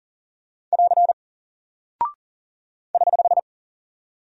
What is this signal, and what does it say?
Morse code, heard as a keyed tone, about 40 wpm, 700 Hz. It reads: <AR> <HH>
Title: Single Character (Letter:Number:Punctuation:Prosign) Alt 40wpm